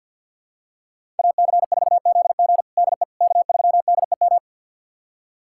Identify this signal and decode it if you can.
Morse code, heard as a keyed tone, about 52 wpm, 700 Hz. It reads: AC4ZG DE K3DEM
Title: USA Callsign DE Callsign - 5 Characters - 50wpm